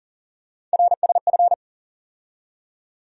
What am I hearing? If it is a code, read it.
Morse code, heard as a keyed tone, about 40 wpm, 700 Hz. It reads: RSF
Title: USA Call Sign Postfixes - 3 Letters 40wpm